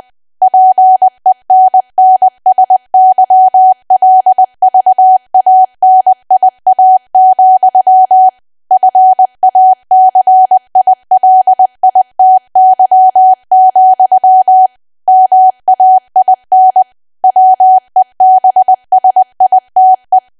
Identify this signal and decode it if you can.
Morse code, heard as a keyed tone, about 20 wpm, 750 Hz. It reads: PENNSYLVANIA, FACILITY, MAIN WEBSITE
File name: cwa / W1AW-Level3 / 20WPM